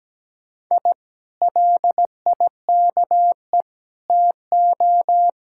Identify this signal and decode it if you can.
Morse code, heard as a keyed tone, about 17 wpm, 700 Hz. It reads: I LIKE TO
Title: Speed-Racing Common QSO Phrases Encore - 17wpm